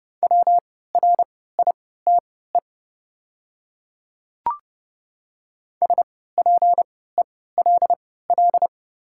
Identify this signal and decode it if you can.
Morse code, heard as a keyed tone, about 30 wpm, spaced as for 17 wpm, 700 Hz. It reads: WRITE SPELL
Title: 3x Character Spacing - Top 200 Words - 30wpm